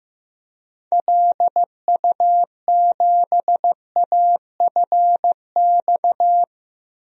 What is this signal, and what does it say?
Morse code, heard as a keyed tone, about 15 wpm, 700 Hz. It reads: LU7AFX
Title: DXCC-100 Call Signs - 6 Characters - 15wpm